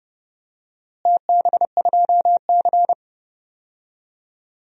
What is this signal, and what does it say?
Morse code, heard as a keyed tone, about 30 wpm, 700 Hz. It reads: TB2C
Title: DXCC-100 Call Signs - 4 Characters - Repeated 2x - 30wpm